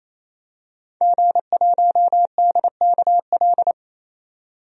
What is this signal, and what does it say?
Morse code, heard as a keyed tone, about 27 wpm, 700 Hz. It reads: G1DKL
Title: DXCC-100 Call Signs - 5 Characters - 28wpm